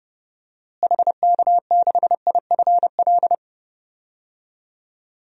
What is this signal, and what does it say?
Morse code, heard as a keyed tone, about 30 wpm, 700 Hz. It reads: HK6IFL
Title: DXCC-100 Call Signs - 6 Characters - Repeated 2x - 30wpm